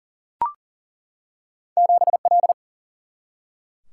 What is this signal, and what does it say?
Morse code, heard as a keyed tone, about 38 wpm, 700 Hz. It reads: 7L
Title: Single-Letter Single-Number 40wpm